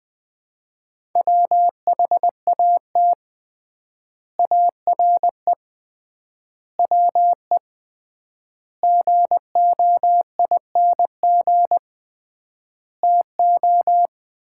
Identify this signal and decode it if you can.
Morse code, heard as a keyed tone, about 20 wpm, 700 Hz. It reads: WHAT ARE WE GOING TO
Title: 3x Word-Spacing - Sets of 5 Words - 20wpm